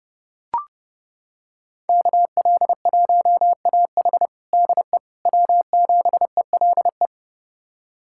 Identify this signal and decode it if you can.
Morse code, heard as a keyed tone, about 30 wpm, 700 Hz. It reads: KL1AH DE W7ELE
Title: USA Callsign DE Callsign 30wpm